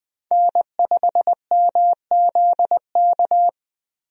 Morse code, heard as a keyed tone, about 20 wpm, 700 Hz. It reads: N5MZK
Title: USA Call Signs - 5 Characters - Repeated 2x - 20wpm